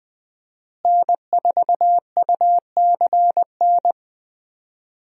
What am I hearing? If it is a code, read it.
Morse code, heard as a keyed tone, about 20 wpm, 700 Hz. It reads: N4UCN
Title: USA Call Signs - 5 Characters - Repeated 3x - 20wpm